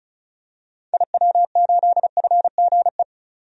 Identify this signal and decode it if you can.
Morse code, heard as a keyed tone, about 36 wpm, 700 Hz. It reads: IW8FGE
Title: DXCC-100 Call Signs - 6 Characters - 35wpm